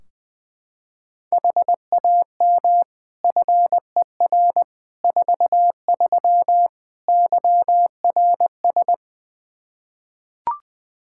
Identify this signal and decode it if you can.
Morse code, heard as a keyed tone, about 20 wpm, 700 Hz. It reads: HAM FER 43 YRS
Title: Practice QSO - Exchange 3 - More Info - 20wpm